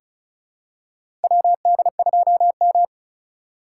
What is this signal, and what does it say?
Morse code, heard as a keyed tone, about 36 wpm, 700 Hz. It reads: WD2M
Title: USA Call Signs - 4 Characters - 3x Character Spacing - 35wpm